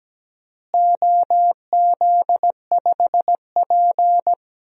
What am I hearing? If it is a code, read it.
Morse code, heard as a keyed tone, about 17 wpm, 700 Hz. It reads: OZ5P
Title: DXCC-100 Call Signs - 4 Characters - Repeated 3x - 17wpm